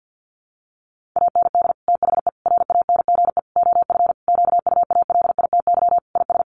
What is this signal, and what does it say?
Morse code, heard as a keyed tone, about 50 wpm, 700 Hz. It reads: AND THE RANGE OF QUALITY IS
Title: Sentences from Top 1500 Words 50wpm